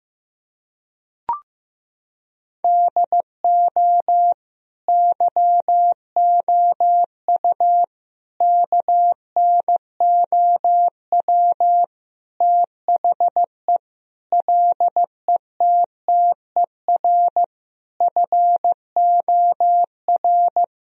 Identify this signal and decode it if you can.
Morse code, heard as a keyed tone, about 15 wpm, 700 Hz. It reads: DO YOU KNOW THE LETTER FOR
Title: Sentences from Top 1000 Words - Review Part 2 - 15wpm